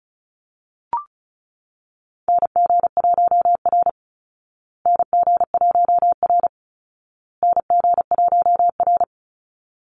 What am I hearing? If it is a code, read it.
Morse code, heard as a keyed tone, about 34 wpm, 700 Hz. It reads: NG1R NG1R NG1R
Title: Call Signs Repeated 2x - 35wpm